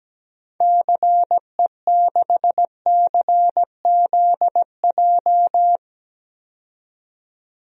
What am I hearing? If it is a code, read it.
Morse code, heard as a keyed tone, about 17 wpm, 700 Hz. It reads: CE6CZJ
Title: DXCC-100 Call Signs - 6 Characters - 17wpm